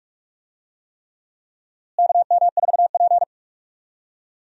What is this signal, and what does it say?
Morse code, heard as a keyed tone, about 42 wpm, 700 Hz. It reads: KM4P
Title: USA Call Signs - 4 Characters - Repeated 3x - 45wpm